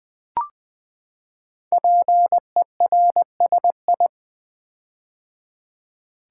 Morse code, heard as a keyed tone, about 20 wpm, 700 Hz. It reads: PERSI
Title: Build A Word - 5 Letters At A Time - 10-Letter Words - 20wpm